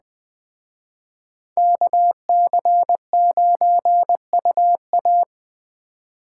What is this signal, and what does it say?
Morse code, heard as a keyed tone, about 20 wpm, 700 Hz. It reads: KC9UA